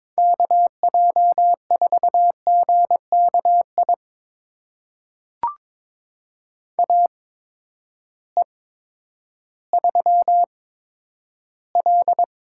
Morse code, heard as a keyed tone, about 22 wpm, spaced as for 11 wpm, 700 Hz. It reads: KJ4GKI AE3L
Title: USA Call Signs - 8x Character Spacing - 22wpm